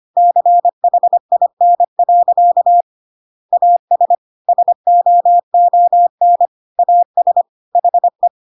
Morse code, heard as a keyed tone, about 25 wpm, 700 Hz. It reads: CHIN. AS SOON AS HE
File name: Pinocchio / Pinocchio ch1 25 25 ik1hge